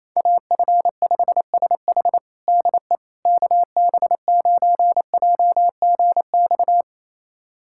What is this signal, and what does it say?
Morse code, heard as a keyed tone, about 28 wpm, 700 Hz. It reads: AF5SH DE KB9JGX
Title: USA Callsign DE Callsign 28wpm